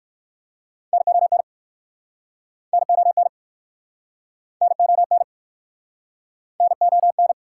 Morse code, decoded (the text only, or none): NON NON NON NON